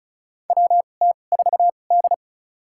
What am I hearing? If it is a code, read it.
Morse code, heard as a keyed tone, about 35 wpm, spaced as for 25 wpm, 700 Hz. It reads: WT4D
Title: USA Call Signs - 4 Characters - 2x Character Spacing - 35wpm